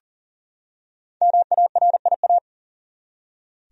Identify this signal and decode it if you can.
Morse code, heard as a keyed tone, about 40 wpm, 700 Hz. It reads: MARIA
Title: USA Common Names - 40wpm